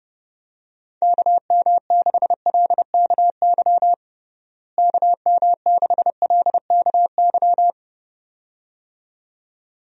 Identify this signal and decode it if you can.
Morse code, heard as a keyed tone, about 32 wpm, 700 Hz. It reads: KM6LKY KM6LKY
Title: USA Call Signs - 6 Characters - Repeated 2x - 30wpm